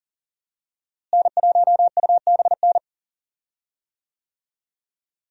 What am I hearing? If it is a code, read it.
Morse code, heard as a keyed tone, about 40 wpm, 700 Hz. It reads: N1UBN
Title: USA Call Signs - 5 Characters - Repeated 2x - 40wpm